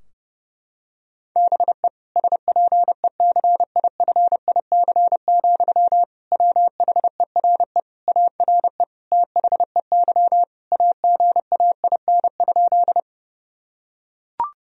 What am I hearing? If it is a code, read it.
Morse code, heard as a keyed tone, about 30 wpm, 700 Hz. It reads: BE SPECIFIC, WHERE ARE THEY AGAIN?
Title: Sentences from Top 1000 Words - Review Part 3 - 30wpm